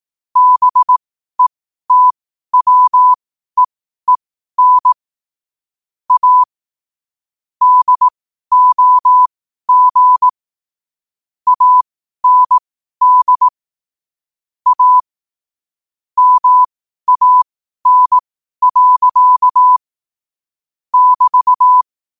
Morse code, heard as a keyed tone, about 18 wpm, spaced as for 12 wpm, 1000 Hz. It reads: BETWEEN A DOG AND A MAN. <BT>
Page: 12 wpm morse code quotes